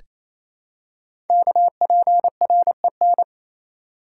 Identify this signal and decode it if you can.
Morse code, heard as a keyed tone, about 28 wpm, 700 Hz. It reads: KPREN
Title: Code Groups - 5 Letters - 28wpm